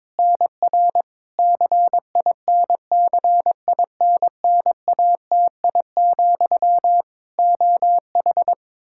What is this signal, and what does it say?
Morse code, heard as a keyed tone, about 22 wpm, 700 Hz. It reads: NR CINCINNATI, OH
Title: K1USN SST QSO Practice 22wpm